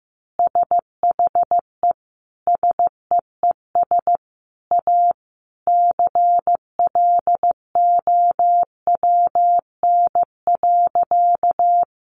Morse code, heard as a keyed tone, about 15 wpm, 700 Hz. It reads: SHE SEES A CLOWN.
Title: Easy Phrases 15wpm